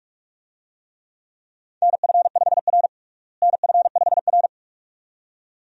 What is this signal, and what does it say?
Morse code, heard as a keyed tone, about 46 wpm, 700 Hz. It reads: NF5R NF5R
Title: USA Call Signs - 4 Characters - Repeated 2x - 45wpm